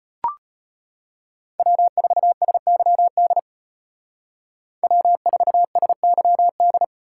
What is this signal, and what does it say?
Morse code, heard as a keyed tone, about 32 wpm, 700 Hz. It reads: W4SYD W4SYD